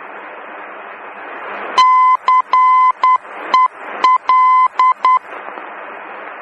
Morse code, heard as a keyed tone, about 10 wpm, 1020 Hz. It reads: CEL